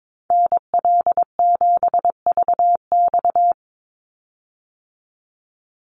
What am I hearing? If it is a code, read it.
Morse code, heard as a keyed tone, about 22 wpm, 700 Hz. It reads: NL7VX
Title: Call Signs - 22wpm